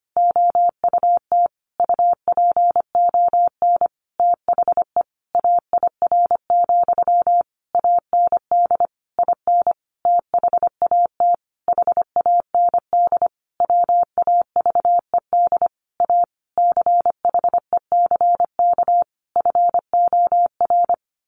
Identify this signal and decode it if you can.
Morse code, heard as a keyed tone, about 25 wpm, 700 Hz. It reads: OUT UPON THE AIR, AND IN THAT HAND WAVED A CHECK FOR
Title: A Sound Of Thunder 25wpm